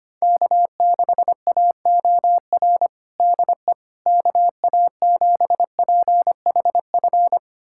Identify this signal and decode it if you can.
Morse code, heard as a keyed tone, about 25 wpm, 700 Hz. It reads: K6AOR DE KA7PHF
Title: USA Callsign DE Callsign 25wpm